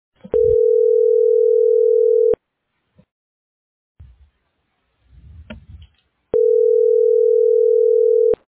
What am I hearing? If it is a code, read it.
Ringback tone (North America), 440 + 480 Hz.